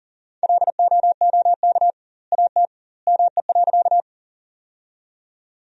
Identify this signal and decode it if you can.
Morse code, heard as a keyed tone, about 40 wpm, 700 Hz. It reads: LOOK AT ME.
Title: 2x Word-Spacing - Sentences from Top 200 Words - 40wpm